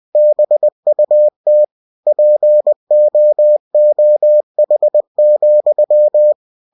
Morse code, heard as a keyed tone, about 20 wpm, 600 Hz. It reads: BUT POOH,
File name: cw-books / winnie-the-pooh / winnie-20wpm-page-0049